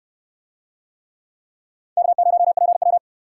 Morse code, heard as a keyed tone, about 64 wpm, 700 Hz. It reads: K0PW